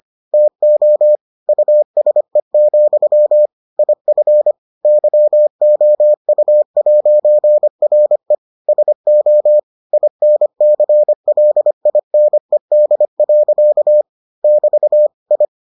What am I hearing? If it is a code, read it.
Morse code, heard as a keyed tone, about 25 wpm, 600 Hz. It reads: TO USE, IF YOU'RE SO INCLINED. <BT> I